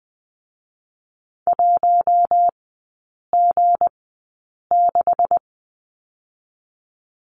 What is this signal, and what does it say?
Morse code, heard as a keyed tone, about 20 wpm, 700 Hz. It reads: 1 G 6
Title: 198 Lesson 32e - Letter k - Three Characters 20wpm